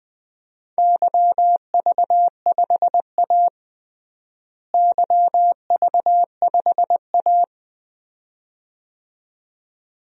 Morse code, heard as a keyed tone, about 20 wpm, 700 Hz. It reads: YV5A YV5A
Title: DXCC-100 Call Signs - 4 Characters - Repeated 2x - 20wpm